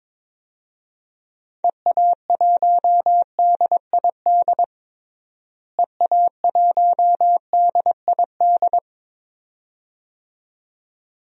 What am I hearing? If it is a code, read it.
Morse code, heard as a keyed tone, about 22 wpm, 700 Hz. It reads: EA1DID EA1DID
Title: DXCC-100 Call Signs - 6 Characters - Repeated 2x - 22wpm